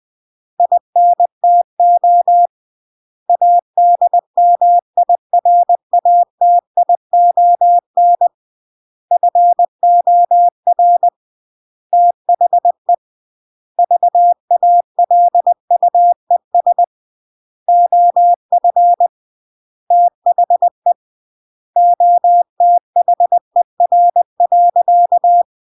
Morse code, heard as a keyed tone, about 20 wpm, 700 Hz. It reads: INTO ADMIRATION FOR THE VALUES OF THE OTHER.